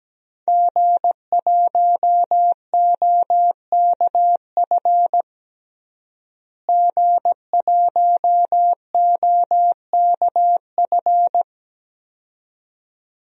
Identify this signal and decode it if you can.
Morse code, heard as a keyed tone, about 17 wpm, 700 Hz. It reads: G1OKF G1OKF
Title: DXCC-100 Call Signs - 5 Characters - Repeated 2x - 17wpm